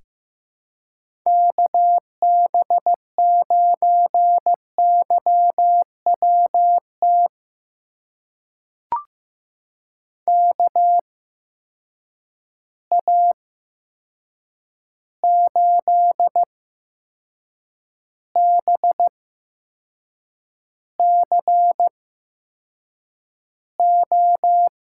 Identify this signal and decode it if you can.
Morse code, heard as a keyed tone, about 15 wpm, spaced as for 6 wpm, 700 Hz. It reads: KB9YWT KA8BCO
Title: USA Call Signs - 6 Characters - 8x Character Spacing - 15wpm